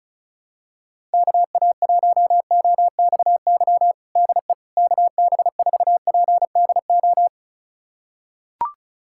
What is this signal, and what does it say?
Morse code, heard as a keyed tone, about 36 wpm, 700 Hz. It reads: KA1OXY DE KB4PDO
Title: USA Callsign DE Callsign - 6 Characters - 35wpm